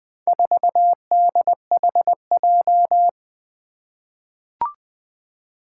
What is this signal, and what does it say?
Morse code, heard as a keyed tone, about 20 wpm, 700 Hz. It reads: 4DHJ
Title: Four Challenging Letter-Number Combinations 20wpm